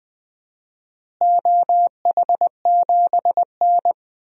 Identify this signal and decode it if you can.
Morse code, heard as a keyed tone, about 21 wpm, 700 Hz. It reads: OH7N
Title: DXCC-100 Call Signs - 4 Characters - Repeated 3x - 20wpm